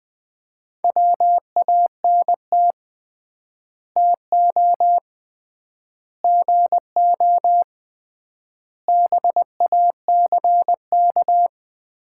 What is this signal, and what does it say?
Morse code, heard as a keyed tone, about 21 wpm, 700 Hz. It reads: WANT TO GO BACK
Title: 3x Word-Spacing - Sets of 5 Words - 20wpm